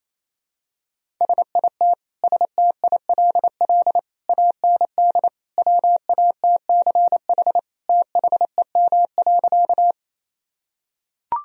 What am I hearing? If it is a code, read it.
Morse code, heard as a keyed tone, about 28 wpm, 700 Hz. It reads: SIT STILL AND WATCH THEM.
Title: Sentences from Top 1000 Words - Review Part 1 - 28wpm